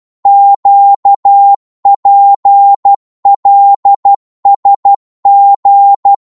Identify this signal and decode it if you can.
Morse code, heard as a keyed tone, about 12 wpm, 800 Hz. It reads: QPLSG
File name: Morse / ltr-12wpm